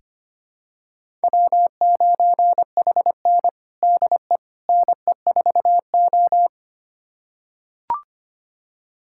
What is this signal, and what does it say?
Morse code, heard as a keyed tone, about 25 wpm, 700 Hz. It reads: W9HN DE NE4O
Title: USA Callsign DE Callsign - 4 Characters - 25wpm